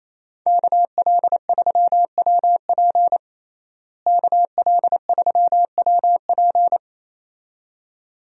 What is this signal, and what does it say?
Morse code, heard as a keyed tone, about 28 wpm, 700 Hz. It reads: KL3WP KL3WP
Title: USA Call Signs - 5 Characters - Repeated 2x - 28wpm